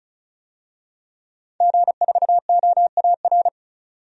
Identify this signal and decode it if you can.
Morse code, heard as a keyed tone, about 34 wpm, 700 Hz. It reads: G4OAR
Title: DXCC-100 Call Signs - 5 Characters - 35wpm